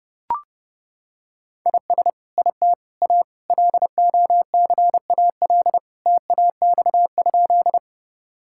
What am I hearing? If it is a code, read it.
Morse code, heard as a keyed tone, about 30 wpm, 700 Hz. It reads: IS IT A LOCAL TAX?
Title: Sentences from Top 1000 Words - Review Part 5 - 30wpm